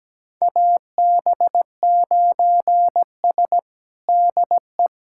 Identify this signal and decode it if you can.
Morse code, heard as a keyed tone, about 17 wpm, 700 Hz. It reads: AB9S DE
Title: USA Callsign DE Callsign - 4 Characters - 17wpm